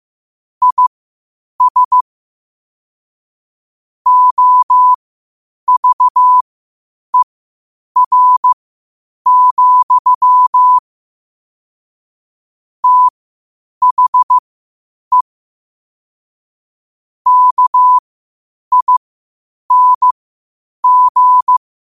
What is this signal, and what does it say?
Morse code, heard as a keyed tone, about 15 wpm, spaced as for 8 wpm, 1000 Hz. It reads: IS OVER, THE KING
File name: qotdfiles / 8wpm